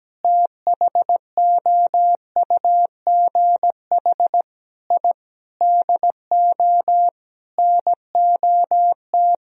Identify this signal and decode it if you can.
Morse code, heard as a keyed tone, about 17 wpm, 700 Hz. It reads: THOUGH I DO NOT
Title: Sentences from Top 1000 Words - Review Part 2 - 17wpm